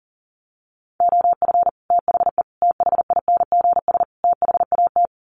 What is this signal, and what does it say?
Morse code, heard as a keyed tone, about 40 wpm, 700 Hz. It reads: OF THE THINGS THAT
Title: Sets of 4 Words 40wpm